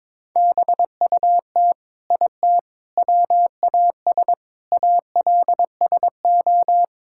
Morse code, heard as a keyed tone, about 22 wpm, 700 Hz. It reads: BUT IT WAS ALSO
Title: Sentences from Top 1600 Words 22wpm